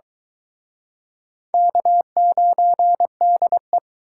Morse code, heard as a keyed tone, about 23 wpm, 700 Hz. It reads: K9DE